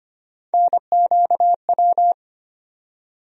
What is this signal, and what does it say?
Morse code, heard as a keyed tone, about 25 wpm, 700 Hz. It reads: NQW
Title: USA Call Sign Postfixes - 3 Letters 25wpm